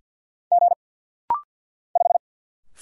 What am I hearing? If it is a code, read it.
Morse code, heard as a keyed tone, about 50 wpm, 700 Hz. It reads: G 5
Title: Single Letter-Number Alt - Rapid-Fire 50wpm